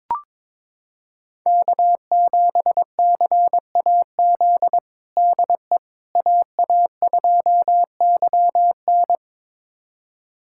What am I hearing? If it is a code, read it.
Morse code, heard as a keyed tone, about 22 wpm, 700 Hz. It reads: K7CAZ DE AA2YN
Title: USA Callsign DE Callsign - 5 Characters - 22wpm